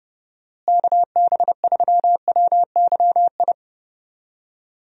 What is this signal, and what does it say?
Morse code, heard as a keyed tone, about 31 wpm, 700 Hz. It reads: KB3WYI
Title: USA Call Signs - 6 Characters - Repeated 3x - 30wpm